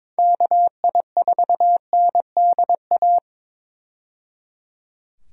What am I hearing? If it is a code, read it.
Morse code, heard as a keyed tone, about 22 wpm, 700 Hz. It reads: KI4NDA
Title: USA Call Signs - 6 Characters - 22wpm